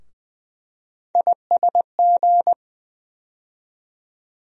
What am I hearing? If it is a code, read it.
Morse code, heard as a keyed tone, about 20 wpm, 700 Hz. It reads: ISG
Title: USA Call Sign Postfixes - 3 Letters 20wpm